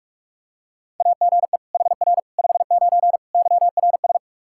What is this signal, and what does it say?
Morse code, heard as a keyed tone, about 46 wpm, 700 Hz. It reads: AGE HR 59 YRS
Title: K1USN SST QSO Practice 45wpm